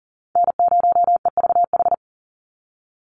Morse code, heard as a keyed tone, about 40 wpm, 700 Hz. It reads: N0EVH
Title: NA SOTA Chaser Callsigns 40wpm